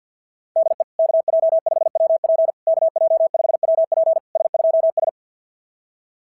Morse code, heard as a keyed tone, about 46 wpm, 660 Hz. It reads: DE KJ5WP KJ5WP S2S
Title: SOTA QSO Practice 50wpm